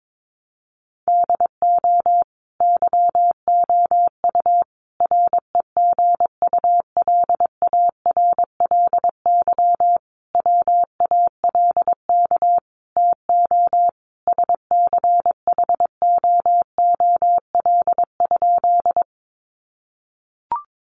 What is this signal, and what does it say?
Morse code, heard as a keyed tone, about 22 wpm, 700 Hz. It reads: DO YOU REGULARLY WALK TO SCHOOL?
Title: Sentences from Top 1400 Words 22wpm